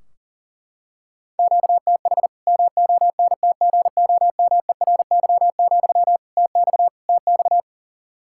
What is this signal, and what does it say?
Morse code, heard as a keyed tone, about 40 wpm, 700 Hz. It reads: QTH MONTGOMERY, TX TX
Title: Practice QSO - Exchange 1 - Beginning - 40wpm